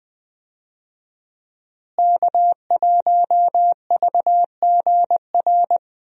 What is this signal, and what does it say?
Morse code, heard as a keyed tone, about 20 wpm, 700 Hz. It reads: K1VGR
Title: USA Call Signs - 5 Characters - Repeated 2x - 20wpm